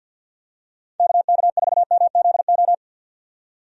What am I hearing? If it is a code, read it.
Morse code, heard as a keyed tone, about 52 wpm, 700 Hz. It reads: KK4MZO
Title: USA Call Signs - 6 Characters - 50wpm